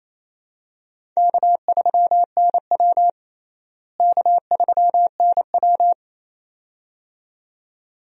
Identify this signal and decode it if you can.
Morse code, heard as a keyed tone, about 27 wpm, 700 Hz. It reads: K3NW K3NW
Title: USA Call Signs - 4 Characters - Repeated 2x - 28wpm